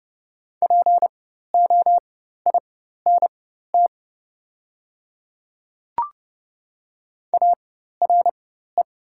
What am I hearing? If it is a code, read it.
Morse code, heard as a keyed tone, about 30 wpm, spaced as for 14 wpm, 700 Hz. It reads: POINT ARE
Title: 4x Character Spacing - Top 200 Words - 30wpm